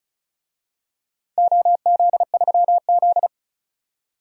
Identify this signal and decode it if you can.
Morse code, heard as a keyed tone, about 36 wpm, 690 Hz. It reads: OZ3Z
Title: DXCC-100 Call Signs - 4 Characters - 35wpm